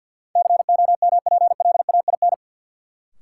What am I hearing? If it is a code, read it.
Morse code, heard as a keyed tone, about 53 wpm, 700 Hz. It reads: COMPLAIN
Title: Sentences from Top 1300 Words - New Word Drill Alt - 50wpm